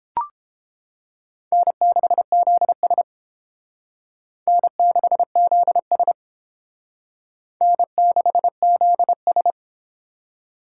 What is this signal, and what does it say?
Morse code, heard as a keyed tone, about 31 wpm, 700 Hz. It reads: N6ZS N6ZS N6ZS